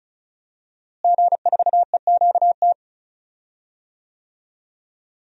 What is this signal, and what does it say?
Morse code, heard as a keyed tone, about 36 wpm, 700 Hz. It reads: G4EQT
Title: DXCC-100 Call Signs - 5 Characters - Repeated 2x - 35wpm